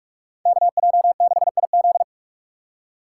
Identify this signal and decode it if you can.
Morse code, heard as a keyed tone, about 43 wpm, 700 Hz. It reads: KJ6IZ
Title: USA Call Signs - 5 Characters - 4x Character Spacing - 45wpm